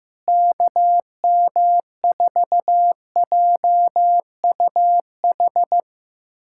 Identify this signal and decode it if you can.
Morse code, heard as a keyed tone, about 15 wpm, 700 Hz. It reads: KM4JUH
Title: USA Callsign DE Callsign 15wpm